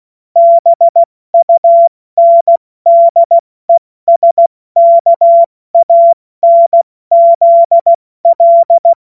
Morse code, heard as a keyed tone, about 16 wpm, 680 Hz. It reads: BUNDESKANZL
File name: Witz-30-September-2025-70-F80